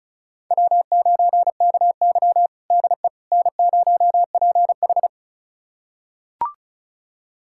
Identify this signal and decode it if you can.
Morse code, heard as a keyed tone, about 36 wpm, 700 Hz. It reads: W9KY DE N0PH
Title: USA Callsign DE Callsign - 4 Characters - 35wpm